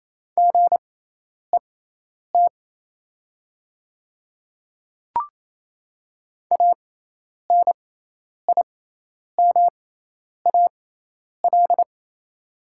Morse code, heard as a keyed tone, about 27 wpm, spaced as for 10 wpm, 690 Hz. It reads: GET ANIMAL
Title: 6x Character Spacing - Top 300 Words - 28wpm